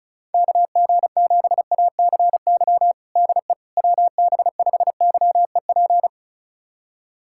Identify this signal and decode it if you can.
Morse code, heard as a keyed tone, about 36 wpm, 700 Hz. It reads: KG7ACY DE WB5YEP
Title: USA Callsign DE Callsign - 6 Characters - 35wpm